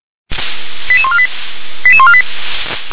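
Five-tone selective call (ZVEI), heard as quick runs of five tones, tones of about 70 ms. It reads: 88148 88148
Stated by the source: Five-tone ZVEI selective call received on SDR.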